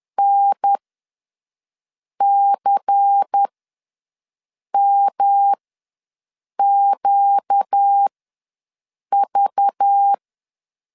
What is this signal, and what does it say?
Morse code, heard as a keyed tone, about 10 wpm, spaced as for 5 wpm, 800 Hz. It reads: NCMQV
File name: LEZIONI / 8 YJ45 / 8 YJ ws 10wpm 4043